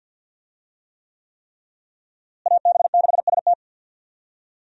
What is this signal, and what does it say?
Morse code, heard as a keyed tone, about 50 wpm, 700 Hz. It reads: AB6ST
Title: USA Call Signs - 5 Characters - Repeated 3x - 50wpm